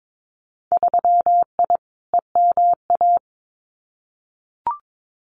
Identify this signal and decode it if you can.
Morse code, heard as a keyed tone, about 22 wpm, 700 Hz. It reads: 3I EMA
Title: Winter Field Day Exchanges Encore 22wpm